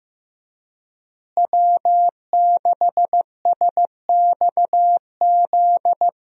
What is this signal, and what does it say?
Morse code, heard as a keyed tone, about 15 wpm, 700 Hz. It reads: W6SXZ
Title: USA Call Signs - 5 Characters - Repeated 3x - 15wpm